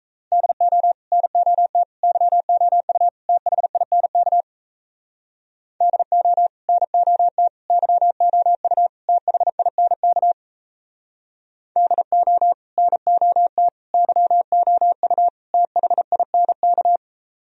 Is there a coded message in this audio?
Yes: Morse code from 0.3 to 17.0 s, heard as a keyed tone, about 36 wpm, 700 Hz. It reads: DO NOT YOU THINK DO NOT YOU THINK DO NOT YOU THINK